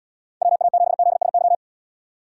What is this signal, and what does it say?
Morse code, heard as a keyed tone, about 79 wpm, 700 Hz. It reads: WA7ZSY